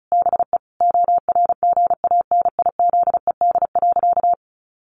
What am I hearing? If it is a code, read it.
Morse code, heard as a keyed tone, about 34 wpm, 700 Hz. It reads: BE ORGANIZED.
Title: Sentences from Top 1300 Words 35wpm